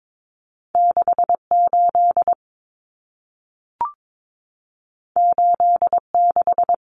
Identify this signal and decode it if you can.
Morse code, heard as a keyed tone, about 22 wpm, 700 Hz. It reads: 68 86
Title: Common Cut Numbers - 2-Digit 22wpm